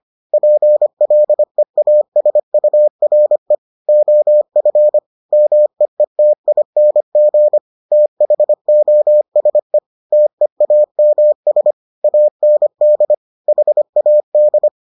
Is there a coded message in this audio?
Yes: Morse code from 0.3 to 14.7 s, heard as a keyed tone, about 25 wpm, 600 Hz. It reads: PLEASURE OF MEETING THOSE TEAMS AND HAD